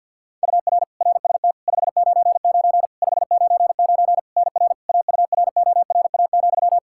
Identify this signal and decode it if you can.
Morse code, heard as a keyed tone, about 52 wpm, 700 Hz. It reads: UR RST 599 599 NR AURORA,
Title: K1USN SST QSO Practice 50wpm